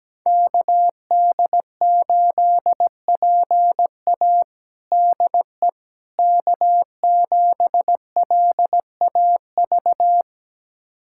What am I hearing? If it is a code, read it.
Morse code, heard as a keyed tone, about 17 wpm, 700 Hz. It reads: KD8PA DE K7LAV
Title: USA Callsign DE Callsign - 5 Characters - 17wpm